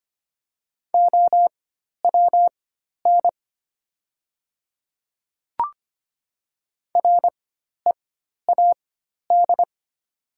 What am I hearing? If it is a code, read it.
Morse code, heard as a keyed tone, about 25 wpm, spaced as for 12 wpm, 700 Hz. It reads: OWN READ
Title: 4x Character Spacing - Top 300 Words - 25wpm